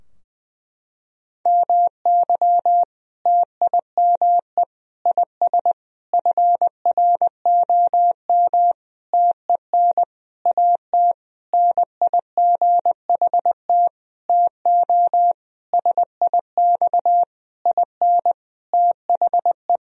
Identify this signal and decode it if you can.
Morse code, heard as a keyed tone, about 21 wpm, 700 Hz. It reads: MY TIME IS FROM TEN AT NIGHT TO SIX IN THE
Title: Sentences from Top 1000 Words - Review Part 2 - 20wpm